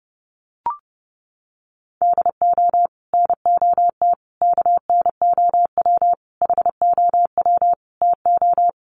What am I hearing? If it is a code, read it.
Morse code, heard as a keyed tone, about 30 wpm, 700 Hz. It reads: DO NOT KNOW HOW TO
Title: Sets of 5 Words 30wpm